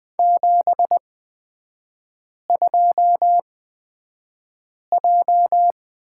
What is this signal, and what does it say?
Morse code, heard as a keyed tone, about 20 wpm, spaced as for 5 wpm, 700 Hz. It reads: 72J
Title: 243 Lesson 39e - Letter z - Three Characters 20wpm